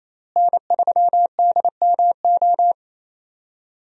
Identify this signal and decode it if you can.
Morse code, heard as a keyed tone, about 27 wpm, 700 Hz. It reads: N3DMO
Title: USA Call Signs - 5 Characters - Repeated 3x - 28wpm